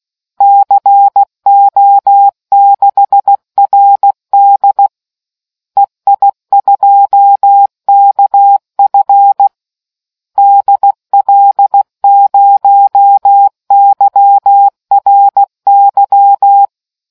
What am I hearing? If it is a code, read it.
Morse code, heard as a keyed tone, about 16 wpm, 800 Hz. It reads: CO6RD EI2KF DL0YRY